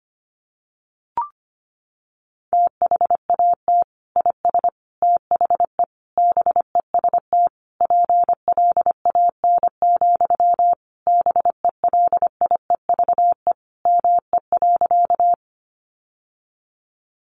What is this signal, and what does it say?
Morse code, heard as a keyed tone, about 25 wpm, 700 Hz. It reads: THAT IS THE BEST PLAN, BELIEVE ME.
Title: Sentences from Top 1000 Words 25wpm